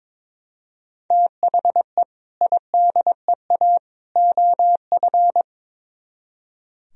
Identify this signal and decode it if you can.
Morse code, heard as a keyed tone, about 24 wpm, 700 Hz. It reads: THE IDEA OF
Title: Speed-Racing - Sets of 3 Words 22wpm